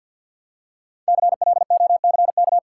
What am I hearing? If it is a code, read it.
Morse code, heard as a keyed tone, about 45 wpm, 700 Hz. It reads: CLOCK
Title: Five Letter Words - 50wpm